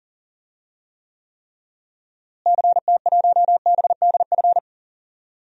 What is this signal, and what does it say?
Morse code, heard as a keyed tone, about 40 wpm, 700 Hz. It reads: CT1BDF
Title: DXCC-100 Call Signs - 6 Characters - Repeated 3x - 40wpm